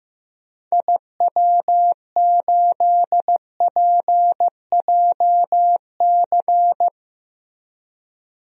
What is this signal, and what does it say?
Morse code, heard as a keyed tone, about 15 wpm, 700 Hz. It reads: IW8PJC
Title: DXCC-100 Call Signs - 6 Characters - Repeated 2x - 15wpm